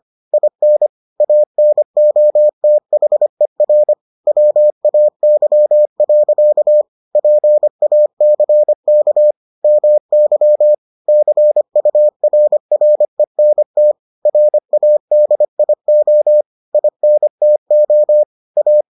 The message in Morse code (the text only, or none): IN ANOTHER WAY. PACK MY CURRENT RADIO INTO A